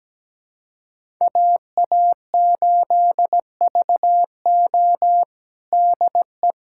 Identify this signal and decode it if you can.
Morse code, heard as a keyed tone, about 17 wpm, 700 Hz. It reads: AA8VO DE
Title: USA Callsign DE Callsign - 5 Characters - 17wpm